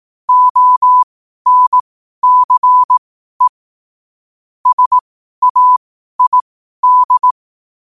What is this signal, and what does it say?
Morse code, heard as a keyed tone, about 18 wpm, spaced as for 12 wpm, 1000 Hz. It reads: ONCE SAID